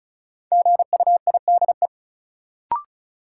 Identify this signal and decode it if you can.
Morse code, heard as a keyed tone, about 36 wpm, 700 Hz. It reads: GUIDE
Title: Sentences from Top 600 Words - New Word Drill - 35wpm